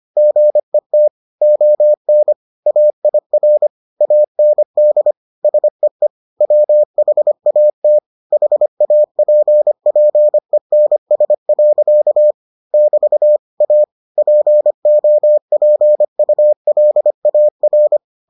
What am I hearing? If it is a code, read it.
Morse code, heard as a keyed tone, about 25 wpm, 600 Hz. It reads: GET ON AIR AND SEE WHAT HAPPENS. <BT> A POPULAR